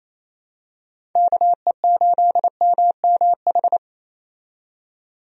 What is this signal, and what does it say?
Morse code, heard as a keyed tone, about 28 wpm, 700 Hz. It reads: KE8MMH
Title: USA Call Signs - 6 Characters - 28wpm